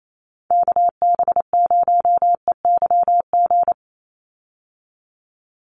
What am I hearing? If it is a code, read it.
Morse code, heard as a keyed tone, about 28 wpm, 700 Hz. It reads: KB0EYG
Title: USA Call Signs 28wpm